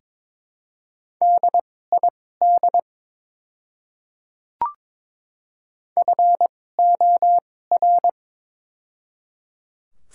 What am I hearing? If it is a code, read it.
Morse code, heard as a keyed tone, about 23 wpm, spaced as for 16 wpm, 700 Hz. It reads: DID FOR
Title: 2x Character Spacing - Top 100 Words - 22wpm